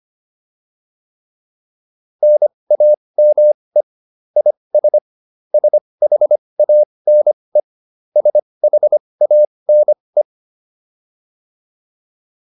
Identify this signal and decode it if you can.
Morse code, heard as a keyed tone, about 25 wpm, spaced as for 20 wpm, 600 Hz. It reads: NAME IS SHANE SHANE
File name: QSO106 20